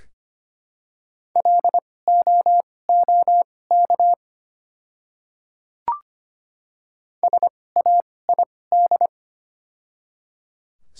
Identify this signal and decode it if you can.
Morse code, heard as a keyed tone, about 25 wpm, spaced as for 18 wpm, 700 Hz. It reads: LOOK SAID
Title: 2x Character Spacing - Top 100 Words - 25wpm